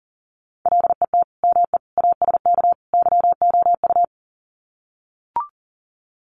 Morse code, heard as a keyed tone, about 40 wpm, 700 Hz. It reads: LET ME ASK YOU
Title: Sets of 4 Words 40wpm